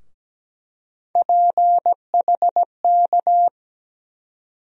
Morse code, heard as a keyed tone, about 17 wpm, 700 Hz. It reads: PHK
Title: USA Call Sign Postfixes - 3 Letters 17wpm